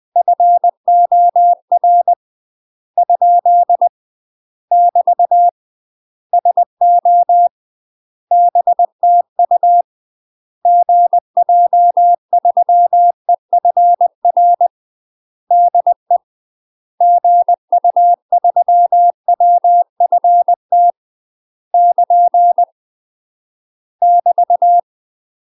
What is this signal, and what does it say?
Morse code, heard as a keyed tone, about 20 wpm, 700 Hz. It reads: FOR ? <BT> SO BTU GJ3EFR DE GU3WFT <KN> <BT>
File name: UK MASTER QSOs TRAINING / 20WPM / UK MASTER QSOs 0031